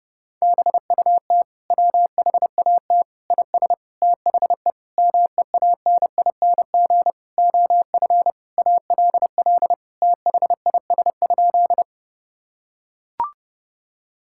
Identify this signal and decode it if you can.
Morse code, heard as a keyed tone, about 30 wpm, 700 Hz. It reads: BUT WHAT IS THE MEANING OF ALL THIS?
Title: Sentences from Top 1000 Words - Review Part 6 - 30wpm